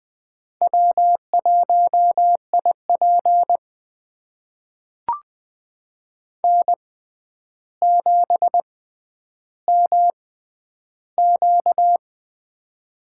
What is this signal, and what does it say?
Morse code, heard as a keyed tone, about 21 wpm, spaced as for 10 wpm, 700 Hz. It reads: W1IP N7MQ
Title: USA Call Signs - 4 Characters - 6x Character Spacing - 20wpm